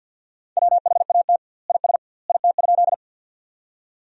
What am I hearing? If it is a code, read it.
Morse code, heard as a keyed tone, about 50 wpm, 700 Hz. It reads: WHAT IS IT?
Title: 2x Word-Spacing - Sentences from Top 100 Words - 50wpm